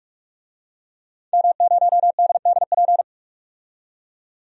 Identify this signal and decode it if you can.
Morse code, heard as a keyed tone, about 46 wpm, 700 Hz. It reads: M0DDP
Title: DXCC-100 Call Signs - 5 Characters - 45wpm